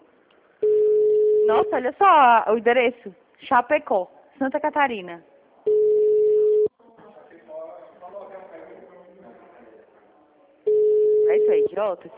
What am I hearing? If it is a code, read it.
Ringback tone (Europe, 425 Hz), with speech.
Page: Brasilia - DF